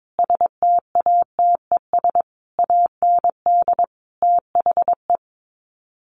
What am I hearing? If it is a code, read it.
Morse code, heard as a keyed tone, about 22 wpm, 700 Hz. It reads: STATES AND THE
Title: Sets of 5 Words 22wpm